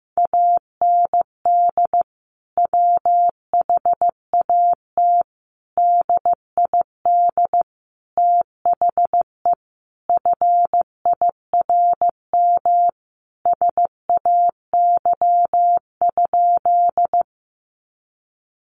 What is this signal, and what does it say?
Morse code, heard as a keyed tone, about 15 wpm, 700 Hz. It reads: AND WHAT DID THE FIRM SAY?
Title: Sentences from Top 1400 Words 15wpm